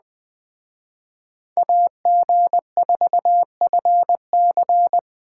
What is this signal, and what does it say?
Morse code, heard as a keyed tone, about 20 wpm, 700 Hz. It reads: AG4FC